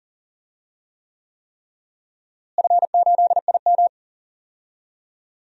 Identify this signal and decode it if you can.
Morse code, heard as a keyed tone, about 40 wpm, 700 Hz. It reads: F8IM
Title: DXCC-100 Call Signs - 4 Characters - Repeated 3x - 40wpm